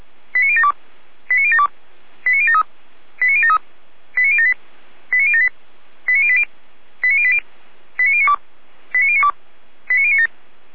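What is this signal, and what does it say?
Five-tone selective call (ZVEI), heard as quick runs of five tones, tones of about 70 ms. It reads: 89072 89072 89073 89073 89078 89078 89080 89080 89092 89092 89097
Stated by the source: ZVEI five-tone call 89050 received on an SDR.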